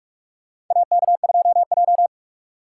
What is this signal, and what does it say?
Morse code, heard as a keyed tone, about 44 wpm, 700 Hz. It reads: AK2J
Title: USA Call Signs - 4 Characters - 4x Character Spacing - 45wpm